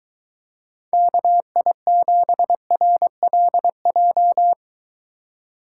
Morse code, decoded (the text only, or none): KI7RLJ